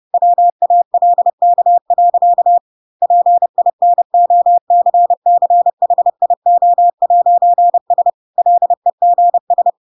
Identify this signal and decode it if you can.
Morse code, heard as a keyed tone, about 30 wpm, 700 Hz. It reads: WALK. PINOCCHIO'S LEGS
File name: Pinocchio / Pinocchio ch3 30 30 ik1hge